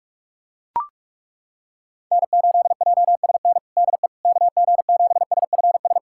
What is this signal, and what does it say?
Morse code, heard as a keyed tone, about 41 wpm, 700 Hz. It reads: N8JSN DE KG7SFS
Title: USA Callsign DE Callsign 45wpm